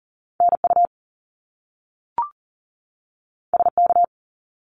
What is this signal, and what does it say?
Morse code, heard as a keyed tone, about 42 wpm, 700 Hz. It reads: NU SK
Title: Canadian Provinces and Territories 40wpm